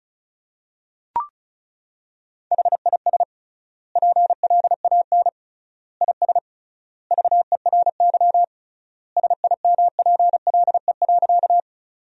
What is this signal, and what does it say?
Morse code, heard as a keyed tone, about 34 wpm, 700 Hz. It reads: HIS PLAN IS VERY SIMPLE.
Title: 3x Word-Spacing - Sentences from Top 500 Words - 35wpm